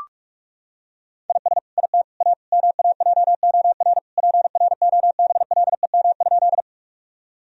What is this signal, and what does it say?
Morse code, heard as a keyed tone, about 41 wpm, 700 Hz. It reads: IS IT A MAJOR PROBLEM?
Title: Sentences from Top 1000 Words - Review Part 4 - 45wpm